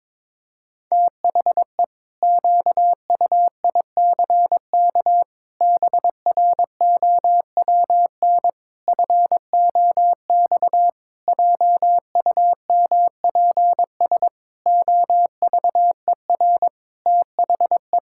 Morse code, heard as a keyed tone, about 22 wpm, 700 Hz. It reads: THE QUICK BROWN FOX JUMPS OVER THE
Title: Pangram Challenge v2 - 22wpm